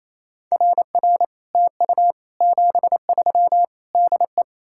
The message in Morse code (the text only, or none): RR TU 73 DE